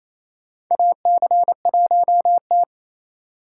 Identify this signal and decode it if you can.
Morse code, heard as a keyed tone, about 28 wpm, 700 Hz. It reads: AC1T
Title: USA Call Signs - 4 Characters - 4x Character Spacing - 28wpm